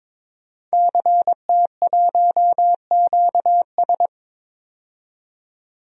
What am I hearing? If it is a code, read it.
Morse code, heard as a keyed tone, about 21 wpm, 700 Hz. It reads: CT1QS
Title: DXCC-100 Call Signs - 5 Characters - 22wpm